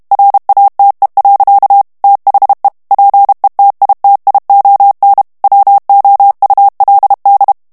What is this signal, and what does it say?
Morse code, heard as a keyed tone, about 32 wpm, 800 Hz. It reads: RATE. THE PETITION WOULD
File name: morse mp3 / ANGLAIS texte clair 3 vitesse 30 mots minute